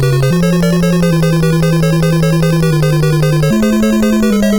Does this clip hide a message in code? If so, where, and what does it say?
No.